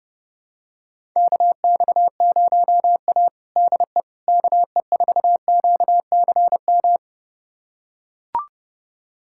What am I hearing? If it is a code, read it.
Morse code, heard as a keyed tone, about 32 wpm, 700 Hz. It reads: KX0A DE KE4QCM
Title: USA Callsign DE Callsign 30wpm